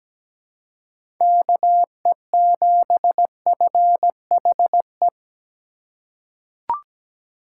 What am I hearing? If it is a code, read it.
Morse code, heard as a keyed tone, about 17 wpm, 700 Hz. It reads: KE7FHE
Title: USA Call Signs - 4x Character Spacing - 17wpm